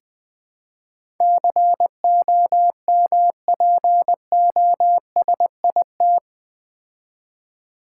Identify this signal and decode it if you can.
Morse code, heard as a keyed tone, about 20 wpm, 700 Hz. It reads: COMPOSIT
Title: Build A Word - 4 Letters At A Time - 11-Letter Words - 20wpm